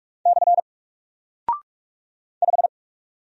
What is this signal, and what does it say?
Morse code, heard as a keyed tone, about 45 wpm, 700 Hz. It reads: / 5
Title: Single Character (Letter:Number:Punctuation:Prosign) Alt 45wpm